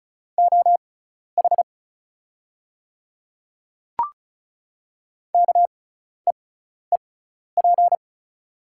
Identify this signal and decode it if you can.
Morse code, heard as a keyed tone, about 36 wpm, spaced as for 12 wpm, 700 Hz. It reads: OH KEEP
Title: 6x Character Spacing - Top 500 Words - 35wpm